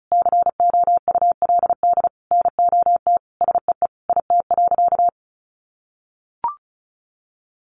Morse code, heard as a keyed tone, about 34 wpm, 700 Hz. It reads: COULD NOT SEE IT.
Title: Sentences from Top 1000 Words 35wpm